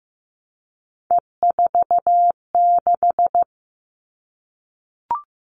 Morse code, heard as a keyed tone, about 15 wpm, 700 Hz. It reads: E46
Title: All Cut Numbers - 3-Digit 15wpm